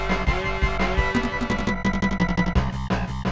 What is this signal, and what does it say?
There is no code.